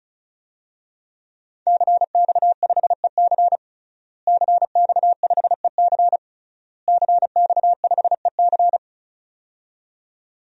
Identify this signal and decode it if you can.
Morse code, heard as a keyed tone, about 36 wpm, 700 Hz. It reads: CX5EC CX5EC CX5EC
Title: DXCC-100 Call Signs - 5 Characters - Repeated 3x - 35wpm